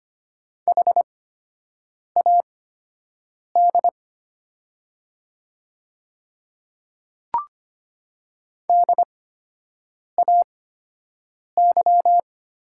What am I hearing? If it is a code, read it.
Morse code, heard as a keyed tone, about 25 wpm, spaced as for 7 wpm, 700 Hz. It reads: HAD DAY
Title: 8x Character Spacing - Top 100 Words - 25wpm